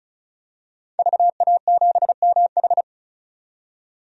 Morse code, heard as a keyed tone, about 36 wpm, 700 Hz. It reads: VA7MH
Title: DXCC-100 Call Signs - 5 Characters - 35wpm